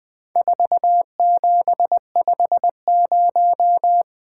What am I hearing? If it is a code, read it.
Morse code, heard as a keyed tone, about 20 wpm, 700 Hz. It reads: 4750
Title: Four Letter-Number Combinations - 20wpm